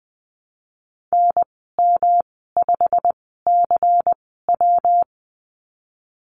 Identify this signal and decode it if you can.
Morse code, heard as a keyed tone, about 20 wpm, spaced as for 15 wpm, 700 Hz. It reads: NM5CW
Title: 140 Lesson 23g - Letter f - Call Signs 20wpm